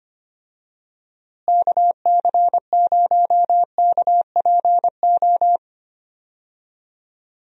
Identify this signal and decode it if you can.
Morse code, heard as a keyed tone, about 25 wpm, 700 Hz. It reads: KC0KPO
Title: USA Call Signs - 6 Characters - Repeated 3x - 25wpm